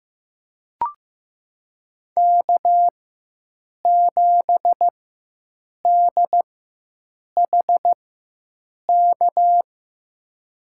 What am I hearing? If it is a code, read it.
Morse code, heard as a keyed tone, about 15 wpm, spaced as for 7 wpm, 700 Hz. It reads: K7DHK
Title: USA Call Signs - 5 Characters - 4x Character Spacing - 15wpm